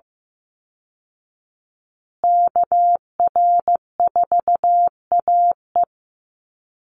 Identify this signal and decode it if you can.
Morse code, heard as a keyed tone, about 15 wpm, 700 Hz. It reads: KR4AE
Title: Call Signs Repeated 2x - 15wpm